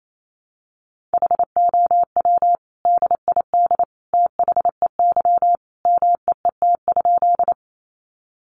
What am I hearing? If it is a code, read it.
Morse code, heard as a keyed tone, about 28 wpm, 700 Hz. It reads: HOW DID THEY MEET?
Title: Sentences from Top 1100 Words 28wpm